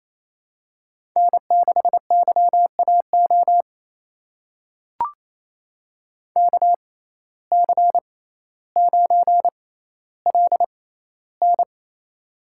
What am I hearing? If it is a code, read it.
Morse code, heard as a keyed tone, about 28 wpm, spaced as for 14 wpm, 700 Hz. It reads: N6YAO KC9LN
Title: USA Call Signs - 5 Characters - 6x Character Spacing - 28wpm